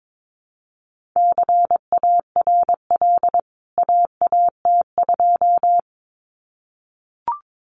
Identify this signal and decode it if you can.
Morse code, heard as a keyed tone, about 22 wpm, 700 Hz. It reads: CARL AAT2
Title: CWT Exchange 22wpm